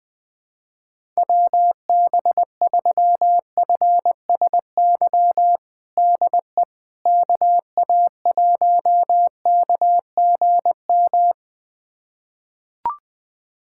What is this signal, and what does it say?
Morse code, heard as a keyed tone, about 20 wpm, 700 Hz. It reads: WB3FSY DE KA1KGM
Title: USA Callsign DE Callsign - 6 Characters - 20wpm